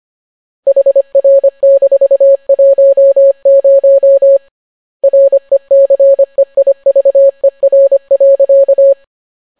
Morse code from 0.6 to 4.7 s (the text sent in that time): HR-10